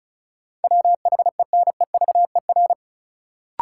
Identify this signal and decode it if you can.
Morse code, heard as a keyed tone, about 34 wpm, 700 Hz. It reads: WHENEVER
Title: Sentences from Top 1200 Words - New Word Drill - 35wpm